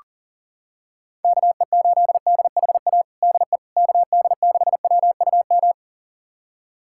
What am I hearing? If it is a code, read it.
Morse code, heard as a keyed tone, about 40 wpm, 700 Hz. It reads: KE8DHA DE KD6WUM
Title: USA Callsign DE Callsign - 6 Characters - 40wpm